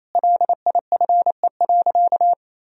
Morse code, heard as a keyed tone, about 27 wpm, 700 Hz. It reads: LIFE.
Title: Sentences from Top 1000 Words - Review Part 4 - 28wpm